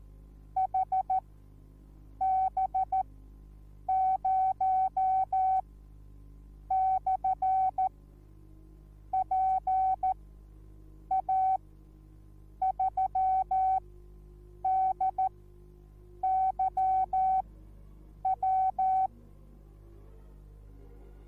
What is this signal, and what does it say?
Morse code, heard as a keyed tone, about 14 wpm, spaced as for 7 wpm, 760 Hz. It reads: HB0/PA3DYW